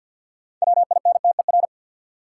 Morse code, heard as a keyed tone, about 49 wpm, 700 Hz. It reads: WINTER
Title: Six Letter Words - 50wpm